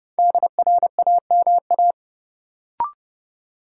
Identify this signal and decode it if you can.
Morse code, heard as a keyed tone, about 30 wpm, 700 Hz. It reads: DRAMA
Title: Sentences from Top 1500 Words - New Word Drill Alt - 30wpm